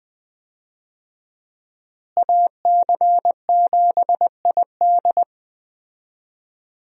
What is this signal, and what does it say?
Morse code, heard as a keyed tone, about 21 wpm, 700 Hz. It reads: AC7ID
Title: USA Call Signs - 5 Characters - Repeated 2x - 20wpm